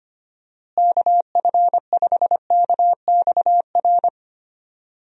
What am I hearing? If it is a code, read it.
Morse code, heard as a keyed tone, about 25 wpm, 700 Hz. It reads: KF5KXR
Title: USA Call Signs - 6 Characters - Repeated 3x - 25wpm